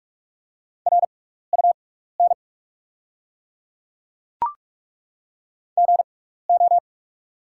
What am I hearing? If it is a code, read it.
Morse code, heard as a keyed tone, about 47 wpm, spaced as for 16 wpm, 690 Hz. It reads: RUN GO
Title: 6x Character Spacing - Top 300 Words - 45wpm